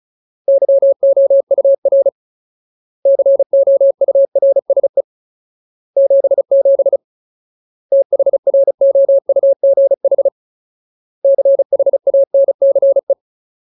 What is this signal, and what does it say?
Morse code, heard as a keyed tone, about 34 wpm, 550 Hz. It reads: YOUR COURSE 77 THROUGH CHANCE
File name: Ultimate Word List / 35WPM Ultimate